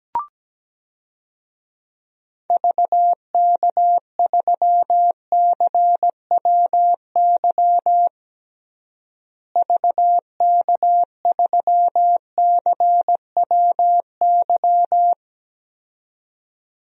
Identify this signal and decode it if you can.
Morse code, heard as a keyed tone, about 17 wpm, 700 Hz. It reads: VK3CWY VK3CWY
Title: DXCC-100 Call Signs - 6 Characters - Repeated 2x - 17wpm